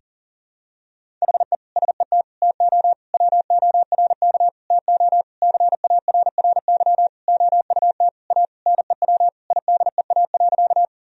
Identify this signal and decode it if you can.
Morse code, heard as a keyed tone, about 40 wpm, 700 Hz. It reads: HE SET TO WORK TO CARRY OUT A NEW IDEA.
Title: Sentences from Top 1000 Words - Review Part 1 - 40wpm